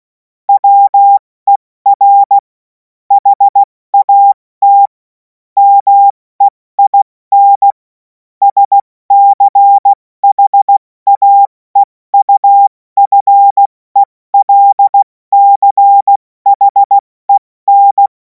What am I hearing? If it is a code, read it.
Morse code, heard as a keyed tone, about 16 wpm, 800 Hz. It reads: WER HAT MEIN SCHAEUFELCHEN
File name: Morsekurs / Witze / Witz 03. März 2026 70 F80